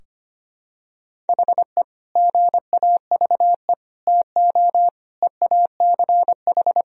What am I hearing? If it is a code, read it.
Morse code, heard as a keyed tone, about 25 wpm, 700 Hz. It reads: HE GAVE TO EACH
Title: Sentences from Top 1000 Words - Review Part 4 - 25wpm